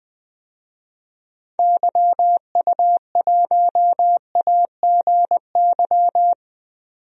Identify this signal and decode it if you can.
Morse code, heard as a keyed tone, about 20 wpm, 700 Hz. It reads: YU1AGY
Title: DXCC-100 Call Signs - 6 Characters - 20wpm